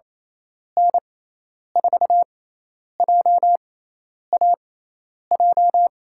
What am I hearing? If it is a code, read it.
Morse code, heard as a keyed tone, about 28 wpm, spaced as for 10 wpm, 700 Hz. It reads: N4JAJ
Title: USA Call Signs - 6 Characters - 6x Character Spacing - 28wpm